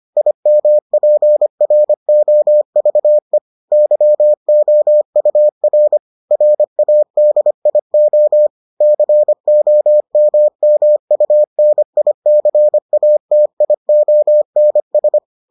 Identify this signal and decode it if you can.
Morse code, heard as a keyed tone, about 25 wpm, 600 Hz. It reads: IMPROVE YOUR RADIO COMMUNICATIONS